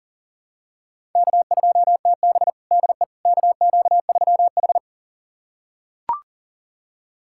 Morse code, heard as a keyed tone, about 40 wpm, 700 Hz. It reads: K2TB DE KQ3H
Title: USA Callsign DE Callsign - 4 Characters - 40wpm